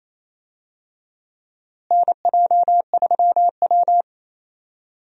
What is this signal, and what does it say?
Morse code, heard as a keyed tone, about 28 wpm, 700 Hz. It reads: NJ3W
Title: USA Call Signs - 4 Characters - Repeated 2x - 28wpm